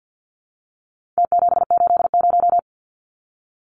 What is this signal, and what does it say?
Morse code, heard as a keyed tone, about 51 wpm, 700 Hz. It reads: T780
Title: Numbers - Contest Style - 4-Digit 50wpm